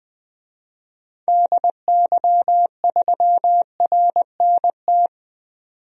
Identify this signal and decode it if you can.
Morse code, heard as a keyed tone, about 20 wpm, 700 Hz. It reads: DY3RNT
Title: DXCC-100 Call Signs - 6 Characters - 20wpm